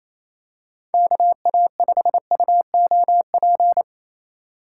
Morse code, heard as a keyed tone, about 28 wpm, 700 Hz. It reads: KA5UOP
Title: USA Call Signs - 6 Characters - 3x Character Spacing - 28wpm